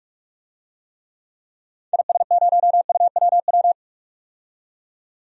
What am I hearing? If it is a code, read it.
Morse code, heard as a keyed tone, about 41 wpm, 700 Hz. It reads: IS0UWW
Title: DXCC-100 Call Signs - 6 Characters - 45wpm